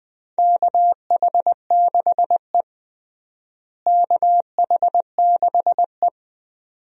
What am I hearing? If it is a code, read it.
Morse code, heard as a keyed tone, about 21 wpm, 700 Hz. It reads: KH6E KH6E
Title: DXCC-100 Call Signs - 4 Characters - Repeated 2x - 20wpm